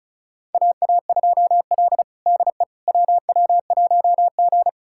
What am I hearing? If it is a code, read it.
Morse code, heard as a keyed tone, about 36 wpm, 700 Hz. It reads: AA2L DE WW1G
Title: USA Callsign DE Callsign - 4 Characters - 35wpm